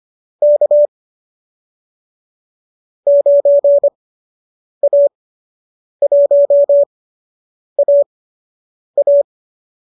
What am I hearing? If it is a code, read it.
Morse code, heard as a keyed tone, about 25 wpm, spaced as for 8 wpm, 570 Hz. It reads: K 9A1AA